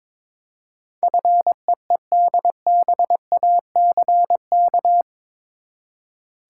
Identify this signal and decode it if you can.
Morse code, heard as a keyed tone, about 22 wpm, 700 Hz. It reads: FEEDBACK
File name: practice / 22wpm / Eight Letter Words - 22wpm